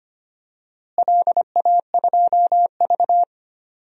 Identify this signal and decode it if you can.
Morse code, heard as a keyed tone, about 25 wpm, 700 Hz. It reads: LA2V
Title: DXCC-100 Call Signs - 4 Characters - Repeated 2x - 25wpm